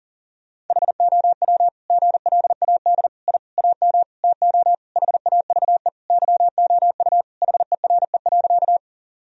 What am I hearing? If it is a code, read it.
Morse code, heard as a keyed tone, about 40 wpm, 700 Hz. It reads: HOW GLAD I AM TO HAVE YOU HERE.
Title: Sentences from Top 1000 Words - Review Part 2 - 40wpm